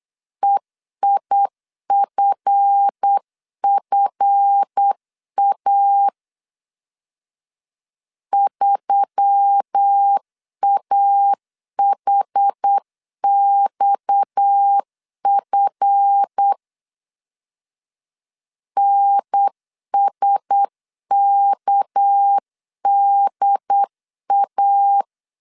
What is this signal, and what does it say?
Morse code, heard as a keyed tone, about 8 wpm, 800 Hz. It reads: EIFFA 3AHXF NSKDA
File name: LEZIONI / 7 GF9Z / 7 GF 8wpm 3108